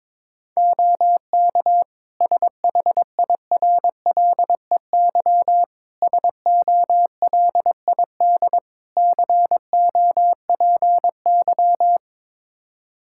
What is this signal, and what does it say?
Morse code, heard as a keyed tone, about 21 wpm, 700 Hz. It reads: OK SHIRLEY SOLID COPY
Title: Practice QSO - Exchange 3 - More Info - 22wpm